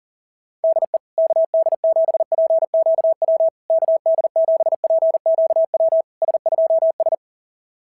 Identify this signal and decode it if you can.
Morse code, heard as a keyed tone, about 40 wpm, 650 Hz. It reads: DE KD7PQW KD7PQW S2S
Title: SOTA QSO Practice 40wpm